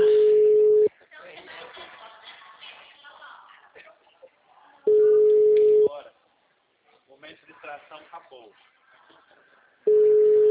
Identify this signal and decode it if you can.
Ringback tone (Europe, 425 Hz), with speech.